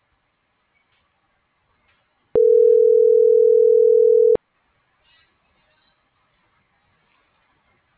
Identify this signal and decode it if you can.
Ringback tone (North America), 440 + 480 Hz.
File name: CENTRAL SOTOVA / out carlos.sotova